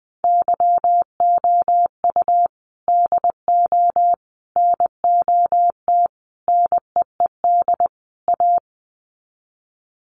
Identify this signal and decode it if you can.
Morse code, heard as a keyed tone, about 19 wpm, 700 Hz. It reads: YOU DO NOT NEED A
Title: Sets of 5 Words Encore 20wpm